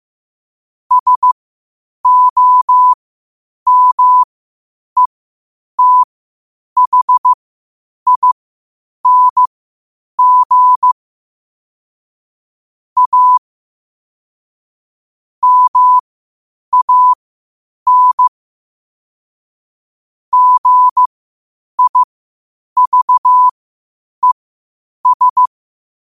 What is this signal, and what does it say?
Morse code, heard as a keyed tone, about 15 wpm, spaced as for 8 wpm, 1000 Hz. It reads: SOMETHING A MAN GIVES
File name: qotdfiles / 8wpm